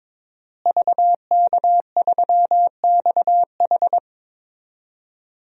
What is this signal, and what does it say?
Morse code, heard as a keyed tone, about 22 wpm, 700 Hz. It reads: VK3XH
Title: DXCC-100 Call Signs - 5 Characters - Repeated 3x - 22wpm